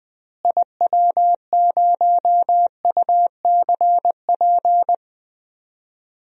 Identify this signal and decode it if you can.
Morse code, heard as a keyed tone, about 20 wpm, 700 Hz. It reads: IW0UCP
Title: DXCC-100 Call Signs - 6 Characters - Repeated 3x - 20wpm